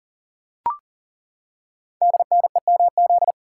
Morse code, heard as a keyed tone, about 39 wpm, 690 Hz. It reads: DNEMZ
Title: Code Groups - 5 Letters - 40wpm